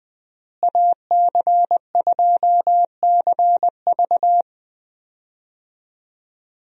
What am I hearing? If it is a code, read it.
Morse code, heard as a keyed tone, about 20 wpm, 700 Hz. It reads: AC2CV
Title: USA Call Signs - 5 Characters - Repeated 3x - 20wpm